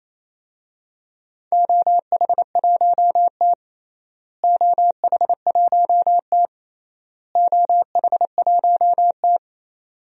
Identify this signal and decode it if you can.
Morse code, heard as a keyed tone, about 28 wpm, 700 Hz. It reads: OH1T OH1T OH1T
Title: DXCC-100 Call Signs - 4 Characters - Repeated 3x - 28wpm